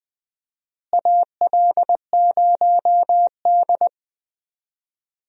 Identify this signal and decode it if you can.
Morse code, heard as a keyed tone, about 20 wpm, 700 Hz. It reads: AL0D
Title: USA Call Signs - 4 Characters - Repeated 2x - 20wpm